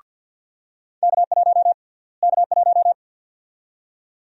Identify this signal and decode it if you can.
Morse code, heard as a keyed tone, about 53 wpm, 700 Hz. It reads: K1 K1
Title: USA Call Sign Prefixes - Repeated 2x - 50wpm